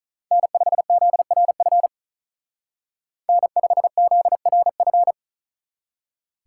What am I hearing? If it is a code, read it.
Morse code, heard as a keyed tone, about 46 wpm, 700 Hz. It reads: N5ZRF N5ZRF